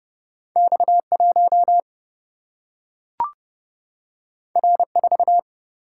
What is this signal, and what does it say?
Morse code, heard as a keyed tone, about 30 wpm, 700 Hz. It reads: X 1 R 4
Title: Single-Letter Single-Number 30wpm